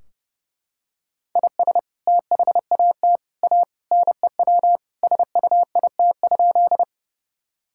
Morse code, heard as a keyed tone, about 30 wpm, 700 Hz. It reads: IS THAT A NEW SUIT?
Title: Sentences from Top 1600 Words 30wpm